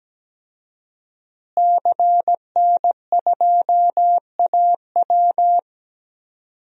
Morse code, heard as a keyed tone, about 17 wpm, 700 Hz. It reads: CN2AW
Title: DXCC-100 Call Signs - 5 Characters - Repeated 2x - 17wpm